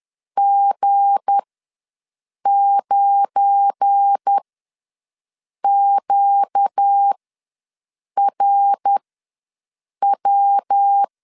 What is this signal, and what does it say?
Morse code, heard as a keyed tone, about 10 wpm, spaced as for 6 wpm, 800 Hz. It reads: G9QRW
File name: LEZIONI / 7 GF9Z / 7 GF9Z ws 10wpm 3919